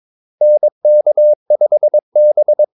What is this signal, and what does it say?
Morse code, heard as a keyed tone, about 22 wpm, 600 Hz. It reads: NK5B
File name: Callsigns-22wpm